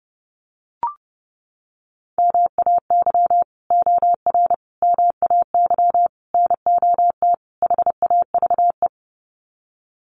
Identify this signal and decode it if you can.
Morse code, heard as a keyed tone, about 30 wpm, 700 Hz. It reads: MAY OR MAY NOT HAVE
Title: Sets of 5 Words Encore2 30wpm